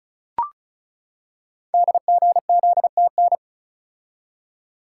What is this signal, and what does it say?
Morse code, heard as a keyed tone, about 35 wpm, 690 Hz. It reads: DGZTN
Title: Code Groups - 5 Letter:Numbers - 35wpm